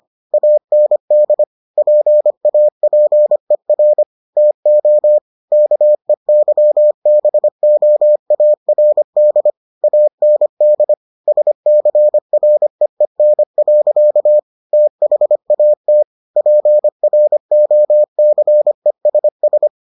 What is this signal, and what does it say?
Morse code, heard as a keyed tone, about 25 wpm, 600 Hz. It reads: AND PAPER TO KEYBOARD AND SCREEN. THAT PROCESS